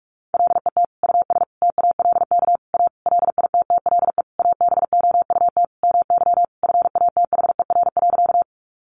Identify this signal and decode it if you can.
Morse code, heard as a keyed tone, about 44 wpm, 700 Hz. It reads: LET US TALK A LITTLE ABOUT MY FATHER.
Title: Sentences from Top 400 Words 45wpm